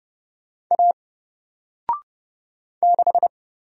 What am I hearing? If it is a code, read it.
Morse code, heard as a keyed tone, about 30 wpm, 700 Hz. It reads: A 6
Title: Single Letter-Number Alt 30wpm